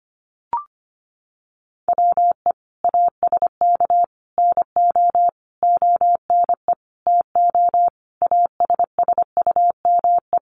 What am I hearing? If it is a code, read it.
Morse code, heard as a keyed tone, about 25 wpm, 700 Hz. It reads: WE ASK NO ONE TO ASSUME
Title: Sentences from Top 1500 Words 25wpm